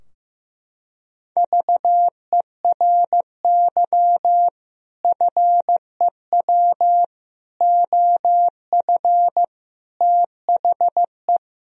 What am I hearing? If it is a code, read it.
Morse code, heard as a keyed tone, about 15 wpm, 700 Hz. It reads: VERY FEW OF THE
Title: Sentences from Top 1000 Words - Review Part 3 - 15wpm